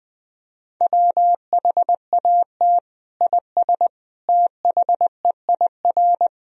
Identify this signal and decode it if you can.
Morse code, heard as a keyed tone, about 21 wpm, 700 Hz. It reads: WHAT IS THEIR
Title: Sentences from Top 1000 Words - Review Part 2 - 20wpm